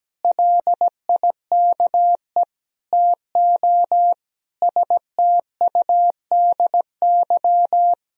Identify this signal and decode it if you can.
Morse code, heard as a keyed tone, about 17 wpm, 700 Hz. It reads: LIKE TO STUDY
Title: Sentences from Top 1000 Words - Review Part 3 - 17wpm